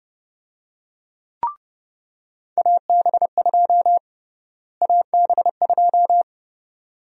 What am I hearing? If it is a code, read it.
Morse code, heard as a keyed tone, about 30 wpm, 700 Hz. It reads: AB2 AB2
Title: USA Call Sign Prefixes - Repeated 2x - 30wpm